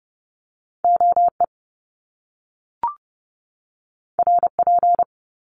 Morse code, heard as a keyed tone, about 30 wpm, 700 Hz. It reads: OE RP
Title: Top Two Letter Combinations in Words 30wpm